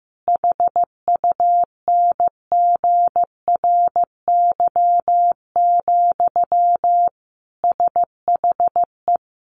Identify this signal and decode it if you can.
Morse code, heard as a keyed tone, about 15 wpm, 700 Hz. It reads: HUNGRY, SHE
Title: Sentences from Top 1400 Words 15wpm